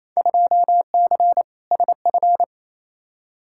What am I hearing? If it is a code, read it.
Morse code, heard as a keyed tone, about 28 wpm, 700 Hz. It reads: 2C SF
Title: Field Day Exchanges Encore - 28wpm